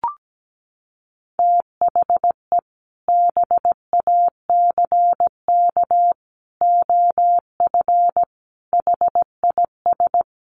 Morse code, heard as a keyed tone, about 17 wpm, 700 Hz. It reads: THE BACK OF HIS
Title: Sets of 5 Words Encore 17wpm